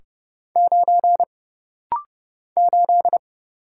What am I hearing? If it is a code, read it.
Morse code, heard as a keyed tone, about 30 wpm, 700 Hz. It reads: 9 8
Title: Single Letter-Number - Rapid-Fire 30wpm